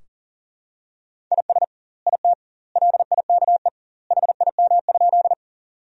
Morse code, heard as a keyed tone, about 40 wpm, 700 Hz. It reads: IS IT LIKE HIM?
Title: 2x Word-Spacing - Sentences from Top 100 Words - 40wpm